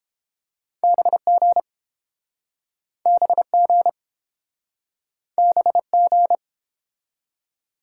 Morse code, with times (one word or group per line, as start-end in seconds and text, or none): 0.8-1.6 BG
3.1-3.9 BG
5.4-6.4 BG